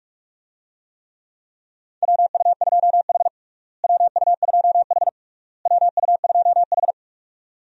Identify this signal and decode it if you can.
Morse code, heard as a keyed tone, about 42 wpm, 700 Hz. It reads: WU2H WU2H WU2H
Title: USA Call Signs - 4 Characters - Repeated 3x - 45wpm